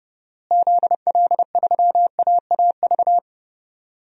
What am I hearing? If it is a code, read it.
Morse code, heard as a keyed tone, about 30 wpm, 700 Hz. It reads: ZL3AAV
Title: DXCC-100 Call Signs - 6 Characters - Repeated 2x - 30wpm